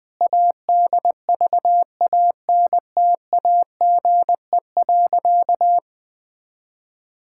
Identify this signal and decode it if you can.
Morse code, heard as a keyed tone, about 20 wpm, 700 Hz. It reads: ADVANTAGE.
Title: Sentences from Top 1000 Words - Review Part 6 - 20wpm